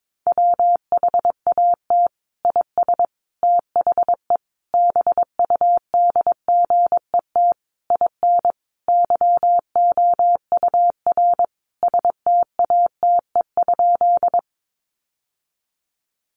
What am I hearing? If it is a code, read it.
Morse code, heard as a keyed tone, about 22 wpm, 700 Hz. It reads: WHAT IS THE BUDGET IN YOUR STATE?
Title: Sentences from Top 1200 Words 22wpm